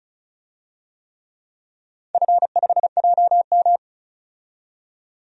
Morse code, heard as a keyed tone, about 34 wpm, 700 Hz. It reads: F5JM
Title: DXCC-100 Call Signs - 4 Characters - 35wpm